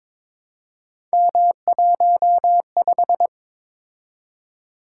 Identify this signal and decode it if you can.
Morse code, heard as a keyed tone, about 23 wpm, 700 Hz. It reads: M15
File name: practice / 22wpm / Three Letter-Number Combinations - 22wpm